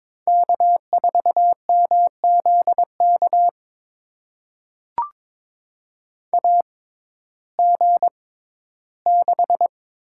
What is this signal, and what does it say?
Morse code, heard as a keyed tone, about 22 wpm, spaced as for 13 wpm, 700 Hz. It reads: K4MZK AG6
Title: USA Call Signs - 5 Characters - 6x Character Spacing - 22wpm